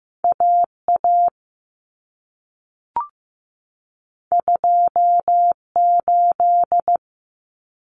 Morse code, heard as a keyed tone, about 15 wpm, 700 Hz. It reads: AA 28
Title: Common Cut Numbers - 2-Digit 15wpm